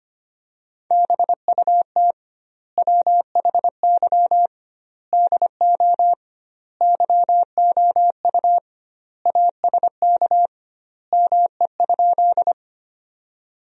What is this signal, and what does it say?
Morse code, heard as a keyed tone, about 25 wpm, 700 Hz. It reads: BUT WHY DO YOU ASK ME?
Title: 2x Word-Spacing - Sentences from Top 200 Words - 25wpm